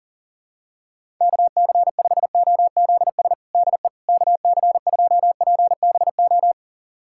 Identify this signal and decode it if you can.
Morse code, heard as a keyed tone, about 37 wpm, 700 Hz. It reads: KC5OZS DE KC2PBO
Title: USA Callsign DE Callsign 40wpm